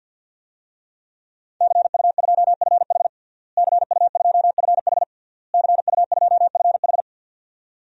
Morse code, heard as a keyed tone, about 51 wpm, 700 Hz. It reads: CU2FH CU2FH CU2FH
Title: DXCC-100 Call Signs - 5 Characters - Repeated 3x - 50wpm